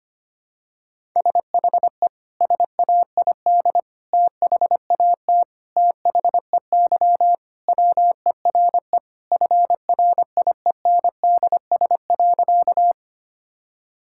Morse code, heard as a keyed tone, about 25 wpm, 700 Hz. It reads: SHE SAID THAT THEY WERE FRIENDS.
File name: practice / 25wpm / Sentences with 3-Word Phrases - 25wpm